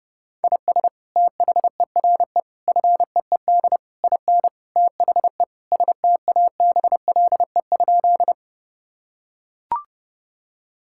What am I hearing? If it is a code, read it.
Morse code, heard as a keyed tone, about 30 wpm, 700 Hz. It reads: IS THERE FEED IN THE STABLE?
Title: Sentences from Top 1600 Words 30wpm